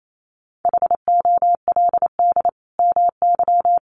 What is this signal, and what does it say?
Morse code, heard as a keyed tone, about 28 wpm, 700 Hz. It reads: HOLD MY
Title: Sentences from Top 1200 Words 28wpm